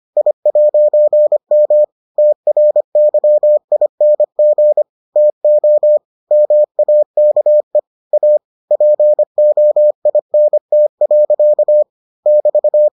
Morse code, heard as a keyed tone, about 25 wpm, 600 Hz. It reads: I'M TRYING TO MAKE A POINT. <BT>